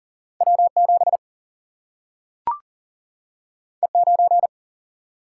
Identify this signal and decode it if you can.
Morse code, heard as a keyed tone, about 40 wpm, 700 Hz. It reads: W 7 E 9
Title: Single-Letter Single-Number 40wpm